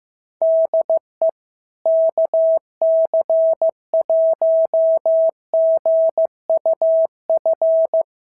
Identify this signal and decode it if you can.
Morse code, heard as a keyed tone, about 15 wpm, 650 Hz. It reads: DE KC1GUF
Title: SOTA QSO Practice 15wpm